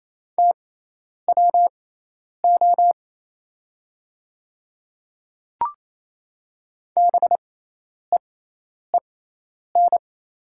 Morse code, heard as a keyed tone, about 28 wpm, spaced as for 10 wpm, 700 Hz. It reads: TWO BEEN
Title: 6x Character Spacing - Top 300 Words - 28wpm